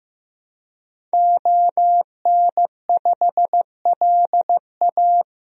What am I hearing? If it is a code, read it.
Morse code, heard as a keyed tone, about 16 wpm, 700 Hz. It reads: ON5LA
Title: DXCC-100 Call Signs - 5 Characters - Repeated 3x - 15wpm